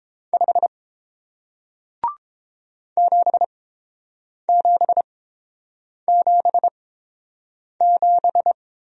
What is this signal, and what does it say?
Morse code, heard as a keyed tone, about 30 wpm, spaced as for 7 wpm, 700 Hz. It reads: 5 7777